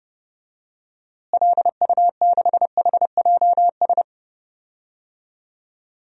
Morse code, heard as a keyed tone, about 30 wpm, 700 Hz. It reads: LU6HJS
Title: DXCC-100 Call Signs - 6 Characters - Repeated 3x - 30wpm